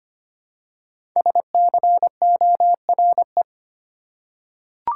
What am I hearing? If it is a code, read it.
Morse code, heard as a keyed tone, about 25 wpm, 700 Hz. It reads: SCORE
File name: 5 Letter Words - First 3 Letters Given - 25wpm